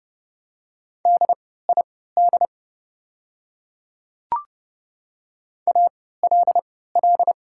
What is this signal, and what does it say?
Morse code, heard as a keyed tone, about 31 wpm, spaced as for 17 wpm, 700 Hz. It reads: DID ALL
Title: 3x Character Spacing - Top 100 Words - 30wpm